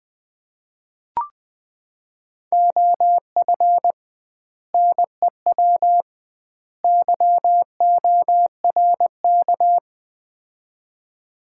Morse code, heard as a keyed tone, about 21 wpm, 700 Hz. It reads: OF NEW YORK
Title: 2x Word-Spacing - Sets of 3 Words - 20wpm